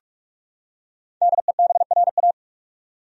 Morse code, heard as a keyed tone, about 46 wpm, 700 Hz. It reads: DEBRA
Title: USA Common Names - Max 5 Letters - 45wpm